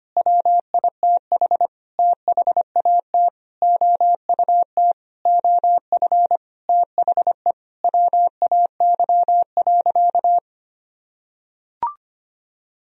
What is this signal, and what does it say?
Morse code, heard as a keyed tone, about 25 wpm, 700 Hz. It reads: WITH THAT OUT OF THE WAY.
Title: Sentences from Top 1000 Words - Review Part 4 - 25wpm